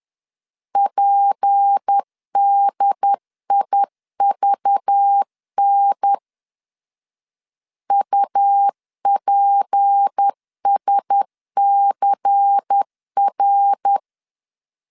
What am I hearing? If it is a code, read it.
Morse code, heard as a keyed tone, about 10 wpm, 800 Hz. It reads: PDIVN UPSCR
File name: LEZIONI / 6 PKX0 / 6 P 10wpm 2432